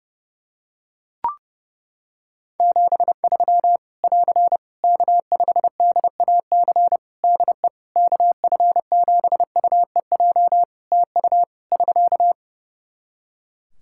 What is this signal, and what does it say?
Morse code, heard as a keyed tone, about 32 wpm, 700 Hz. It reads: 73 <AR> K5DAC DE KF7UEJ TU <SK>
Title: Practice QSO - Exchange 4 - Ending - 30wpm